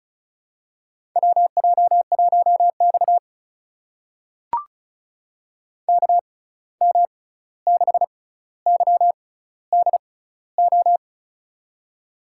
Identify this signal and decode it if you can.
Morse code, heard as a keyed tone, about 36 wpm, spaced as for 16 wpm, 700 Hz. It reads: WJ1X KM6YDO
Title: USA Call Signs - 6x Character Spacing - 35wpm